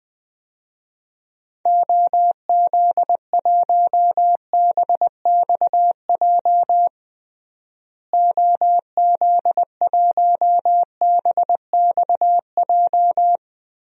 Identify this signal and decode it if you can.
Morse code, heard as a keyed tone, about 20 wpm, 700 Hz. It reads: OZ1BXJ OZ1BXJ
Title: DXCC-100 Call Signs - 6 Characters - Repeated 3x - 20wpm